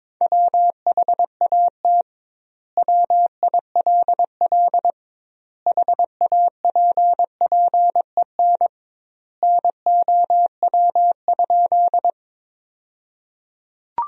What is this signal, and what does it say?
Morse code, heard as a keyed tone, about 23 wpm, 700 Hz. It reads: WHAT WILL HAPPEN NOW?
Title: 2x Word-Spacing - Sentences from Top 400 Words - 22wpm